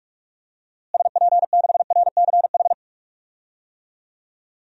Morse code, heard as a keyed tone, about 41 wpm, 700 Hz. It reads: SP6RCH
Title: DXCC-100 Call Signs - 6 Characters - Repeated 3x - 45wpm